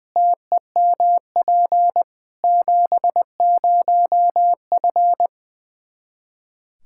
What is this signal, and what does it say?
Morse code, heard as a keyed tone, about 21 wpm, 700 Hz. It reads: TEMP 70F
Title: Practice QSO - Exchange 2 - Info - 20wpm